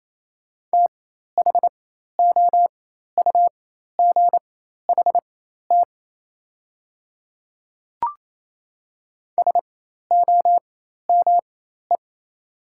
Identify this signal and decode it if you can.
Morse code, heard as a keyed tone, about 28 wpm, spaced as for 13 wpm, 690 Hz. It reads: THOUGHT SOME
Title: 4x Character Spacing - Top 300 Words - 28wpm